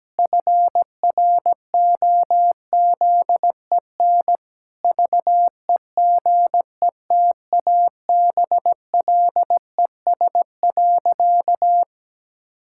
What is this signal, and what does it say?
Morse code, heard as a keyed tone, about 17 wpm, 700 Hz. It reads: FROZEN VEGETABLES.
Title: Pangram Challenge v2 - 17wpm